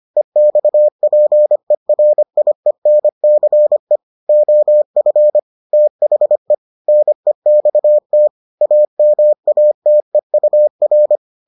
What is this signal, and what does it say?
Morse code, heard as a keyed tone, about 25 wpm, 600 Hz. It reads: EXPERIENCE OF THE NEXT AMATEUR